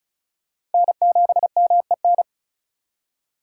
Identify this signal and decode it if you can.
Morse code, heard as a keyed tone, about 33 wpm, 690 Hz. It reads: N7MEN
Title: USA Call Signs - 5 Characters - 3x Character Spacing - 35wpm